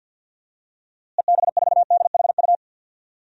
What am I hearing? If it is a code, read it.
Morse code, heard as a keyed tone, about 53 wpm, 700 Hz. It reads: EB4DHU
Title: DXCC-100 Call Signs - 6 Characters - Repeated 2x - 50wpm